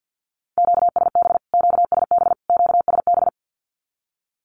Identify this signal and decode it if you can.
Morse code, heard as a keyed tone, about 50 wpm, 700 Hz. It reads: QSB QSB QSB
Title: Q-Codes Repeated 3x 50wpm